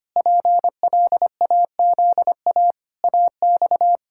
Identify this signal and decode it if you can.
Morse code, heard as a keyed tone, about 25 wpm, 700 Hz. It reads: PLAZA AX
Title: Pangram Challenge v2 - 25wpm